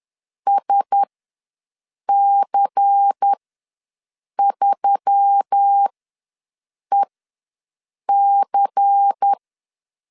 Morse code, heard as a keyed tone, about 10 wpm, spaced as for 6 wpm, 800 Hz. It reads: SC3EC
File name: LEZIONI / 5 LBVC / 5 LBVC ws 10wpm 3733